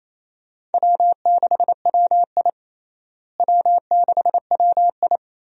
Morse code, heard as a keyed tone, about 28 wpm, 700 Hz. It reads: W6WI W6WI
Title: USA Call Signs - 4 Characters - Repeated 2x - 28wpm